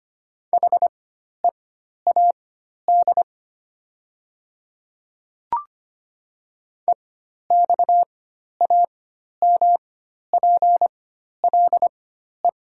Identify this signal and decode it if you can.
Morse code, heard as a keyed tone, about 25 wpm, spaced as for 12 wpm, 700 Hz. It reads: HEAD EXAMPLE
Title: 4x Character Spacing - Top 500 Words - 25wpm